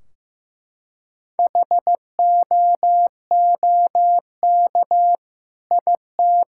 Morse code, heard as a keyed tone, about 15 wpm, 700 Hz. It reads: HOOK IT
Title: Sentences from Top 1600 Words 15wpm